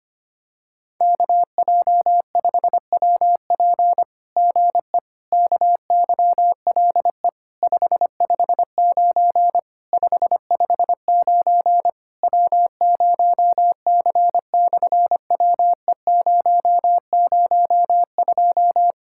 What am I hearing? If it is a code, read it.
Morse code, heard as a keyed tone, about 25 wpm, 700 Hz. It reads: KJ5WP GE KYLE 559 559 W0C/WE002
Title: SOTA QSO Practice 25wpm